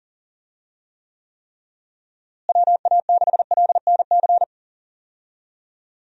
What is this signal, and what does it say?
Morse code, heard as a keyed tone, about 40 wpm, 700 Hz. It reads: WA6LNC
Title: USA Call Signs - 6 Characters - Repeated 3x - 40wpm